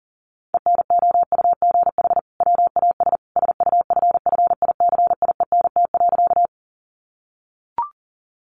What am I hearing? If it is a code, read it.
Morse code, heard as a keyed tone, about 40 wpm, 700 Hz. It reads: ENOUGH WAS SUFFICIENT.
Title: Sentences from Top 1400 Words 40wpm